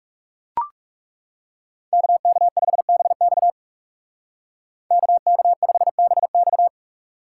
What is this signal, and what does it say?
Morse code, heard as a keyed tone, about 36 wpm, 700 Hz. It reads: KK5BX KK5BX
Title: Speed Racing - USA Call Signs 30wpm